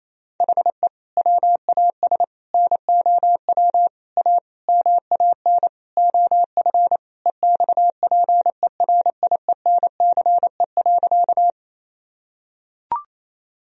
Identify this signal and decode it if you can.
Morse code, heard as a keyed tone, about 28 wpm, 700 Hz. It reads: HE WAS NOW A MAN OF EXPERIENCE.
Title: Sentences from Top 1000 Words - Review Part 3 - 28wpm